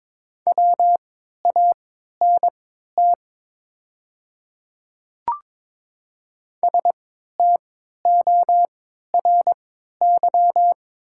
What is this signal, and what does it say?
Morse code, heard as a keyed tone, about 22 wpm, spaced as for 12 wpm, 700 Hz. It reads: WANT STORY
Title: 3x Character Spacing - Top 300 Words - 22wpm